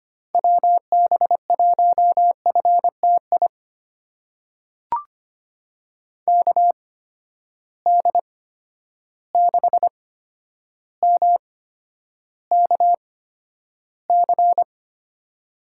Morse code, heard as a keyed tone, about 25 wpm, spaced as for 11 wpm, 700 Hz. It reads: WB1FTI KD6MKC
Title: USA Call Signs - 6 Characters - 8x Character Spacing - 25wpm